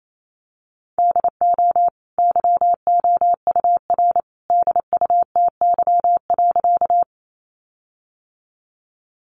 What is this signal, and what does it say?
Morse code, heard as a keyed tone, about 28 wpm, 700 Hz. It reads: DO YOUR DUTY.
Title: Sentences from Top 1500 Words 28wpm